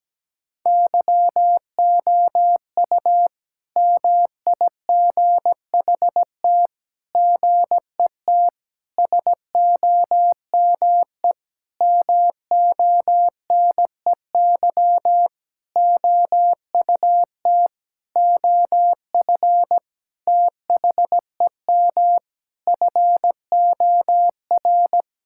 Morse code, heard as a keyed tone, about 17 wpm, 700 Hz. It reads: YOU MIGHT GET SOME MONEY OUT OF THEM FOR
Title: Sentences from Top 1000 Words - Review Part 5 - 17wpm